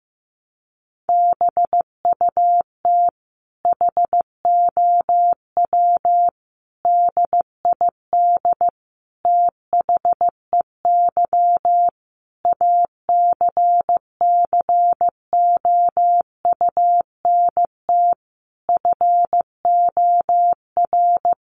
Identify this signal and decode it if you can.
Morse code, heard as a keyed tone, about 15 wpm, 700 Hz. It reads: BUT HOW DID THEY ACCOUNT FOR
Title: Sentences from Top 900 Words 15wpm